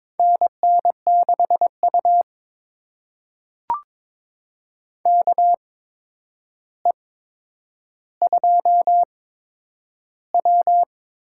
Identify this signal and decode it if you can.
Morse code, heard as a keyed tone, about 22 wpm, spaced as for 9 wpm, 700 Hz. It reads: NN6U KE2W
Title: USA Call Signs - 4 Characters - 8x Character Spacing - 22wpm